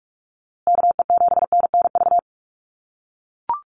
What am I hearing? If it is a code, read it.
Morse code, heard as a keyed tone, about 44 wpm, 700 Hz. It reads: KE7NNV
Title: USA Call Signs Repeated 2x 45wpm